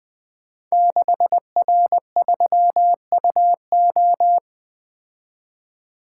Morse code, heard as a keyed tone, about 20 wpm, 700 Hz. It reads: 6R3UO
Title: Code Groups - 5 Letter:Numbers - 20wpm